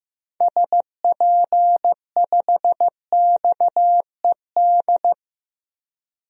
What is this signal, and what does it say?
Morse code, heard as a keyed tone, about 15 wpm, 700 Hz. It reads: SP5XED
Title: DXCC-100 Call Signs - 6 Characters - 15wpm